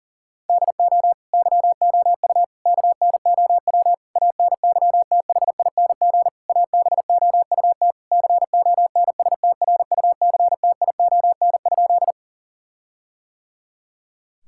Morse code, heard as a keyed tone, about 40 wpm, 700 Hz. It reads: DO YOU KNOW ANYTHING ABOUT CONSTRUCTION?
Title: Sentences from Top 1600 Words 40wpm